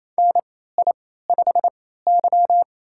Morse code, heard as a keyed tone, about 28 wpm, spaced as for 16 wpm, 700 Hz. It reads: NI5Y
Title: USA Call Signs - 4 Characters - 3x Character Spacing - 28wpm